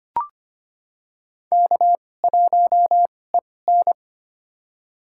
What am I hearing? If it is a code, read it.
Morse code, heard as a keyed tone, about 25 wpm, spaced as for 18 wpm, 700 Hz. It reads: K1EN
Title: USA Call Signs - 4 Characters - 2x Character Spacing - 25wpm